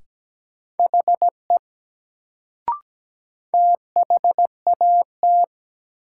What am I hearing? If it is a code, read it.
Morse code, heard as a keyed tone, about 17 wpm, 700 Hz. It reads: HE THAT
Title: Sentences from Top 100 Words - New Word Drill - 17wpm